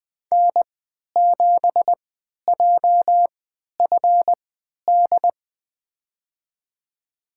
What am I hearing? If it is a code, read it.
Morse code, heard as a keyed tone, about 21 wpm, spaced as for 11 wpm, 700 Hz. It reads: N7JFD
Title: USA Call Signs - 3x Character Spacing - 20wpm